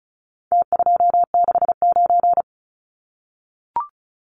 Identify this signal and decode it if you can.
Morse code, heard as a keyed tone, about 34 wpm, 700 Hz. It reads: T269
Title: Numbers - Contest Style - 4-Digit 35wpm